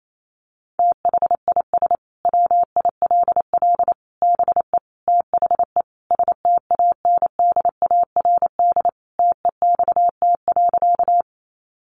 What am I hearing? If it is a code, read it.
Morse code, heard as a keyed tone, about 28 wpm, 700 Hz. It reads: THIS WILL BE THE STANDARD TEXT.
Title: Sentences from Top 1200 Words 28wpm